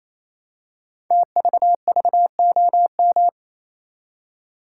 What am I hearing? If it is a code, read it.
Morse code, heard as a keyed tone, about 27 wpm, 700 Hz. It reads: TVVOM
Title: Code Groups - 5 Letters - 28wpm